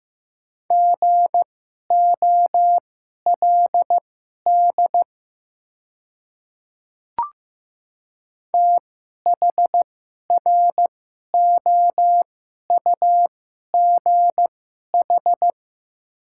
Morse code, heard as a keyed tone, about 15 wpm, spaced as for 11 wpm, 700 Hz. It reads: GOLD THROUGH
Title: 2x Character Spacing - Top 500 Words - 15wpm